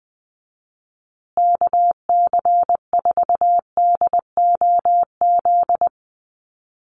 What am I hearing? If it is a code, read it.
Morse code, heard as a keyed tone, about 20 wpm, 700 Hz. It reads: KC4DOZ
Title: USA Call Signs Repeated 3x 20wpm